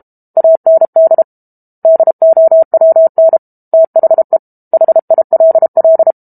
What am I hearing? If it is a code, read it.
Morse code, heard as a keyed tone, about 32 wpm, 650 Hz. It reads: AND DOWN THE HILL
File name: Book Two / 06-Chapter 6 - 32 WPM